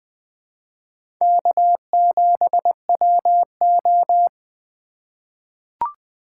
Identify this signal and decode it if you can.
Morse code, heard as a keyed tone, about 20 wpm, 700 Hz. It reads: K7WO
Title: USA Call Signs - 3x Character Spacing - 20wpm